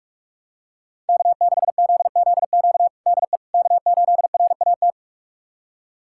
Morse code, heard as a keyed tone, about 42 wpm, 700 Hz. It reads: K6ZZQ DE K8RAT
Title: USA Callsign DE Callsign 45wpm